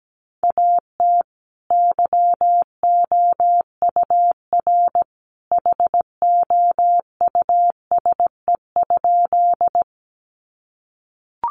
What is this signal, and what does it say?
Morse code, heard as a keyed tone, about 17 wpm, 700 Hz. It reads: AT YOUR HOUSE?
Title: Sentences from Top 200 Words 17wpm